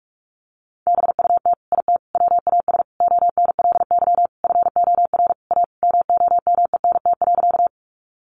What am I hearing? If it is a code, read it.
Morse code, heard as a keyed tone, about 46 wpm, 700 Hz. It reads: BUT IT WAS ONLY FOR A MOMENT.
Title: Sentences from Top 1000 Words 45wpm